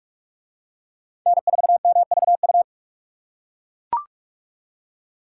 Morse code, heard as a keyed tone, about 40 wpm, 700 Hz. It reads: N4MVU